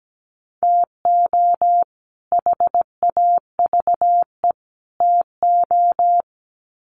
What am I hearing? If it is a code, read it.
Morse code, heard as a keyed tone, about 17 wpm, 700 Hz. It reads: TO HAVE TO
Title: Sets of 3 Words 17wpm